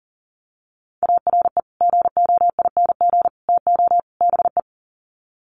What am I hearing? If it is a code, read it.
Morse code, heard as a keyed tone, about 40 wpm, 700 Hz. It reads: ARE GOING TO BE
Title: Sets of 4 Words 40wpm